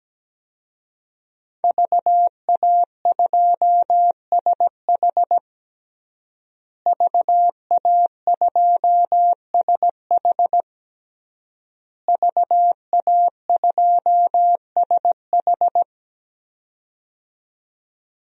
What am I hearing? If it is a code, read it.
Morse code, heard as a keyed tone, about 17 wpm, 700 Hz. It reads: VA2SH VA2SH VA2SH
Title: DXCC-100 Call Signs - 5 Characters - Repeated 3x - 17wpm